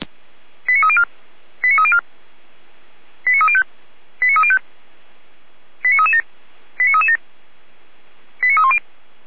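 Five-tone selective call (ZVEI), heard as quick runs of five tones, tones of about 70 ms. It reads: 89304 89304 89305 89305 89337 89337 89310
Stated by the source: ZVEI five-tone selective call 89262 received by SDR, logged by feuerwehr-orlamuende fire brigade.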